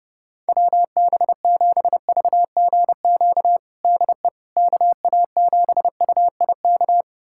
Morse code, heard as a keyed tone, about 32 wpm, 700 Hz. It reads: WB7VGQ DE KA7UIK
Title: USA Callsign DE Callsign - 6 Characters - 30wpm